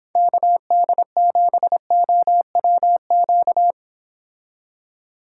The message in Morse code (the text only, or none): KD7OWQ